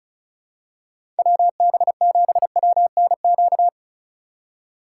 Morse code, heard as a keyed tone, about 34 wpm, 700 Hz. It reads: WB7WNQ
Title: USA Call Signs - 6 Characters - 4x Character Spacing - 35wpm